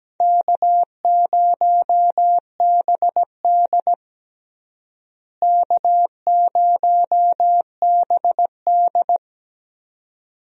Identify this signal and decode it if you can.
Morse code, heard as a keyed tone, about 17 wpm, 700 Hz. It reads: K0BD K0BD
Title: USA Call Signs - 4 Characters - Repeated 2x - 17wpm